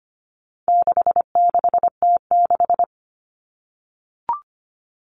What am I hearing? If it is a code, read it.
Morse code, heard as a keyed tone, about 25 wpm, 700 Hz. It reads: 66T6
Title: Common Cut Numbers - 4-Digit 25wpm